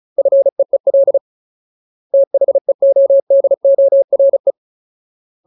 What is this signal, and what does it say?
Morse code, heard as a keyed tone, about 34 wpm, 550 Hz. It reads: FEEL THEODORE
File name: Ultimate Word List / 35WPM Ultimate